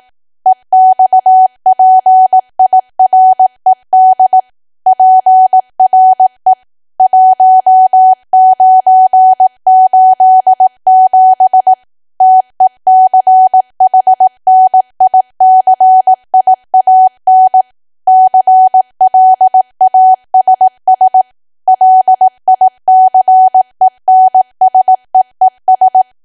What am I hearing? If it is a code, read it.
Morse code, heard as a keyed tone, about 18 wpm, 750 Hz. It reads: EXPIRED PRE 1987 TECHNICIAN CLASS LICENSEES
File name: cwa / W1AW-Level2 / 18WPM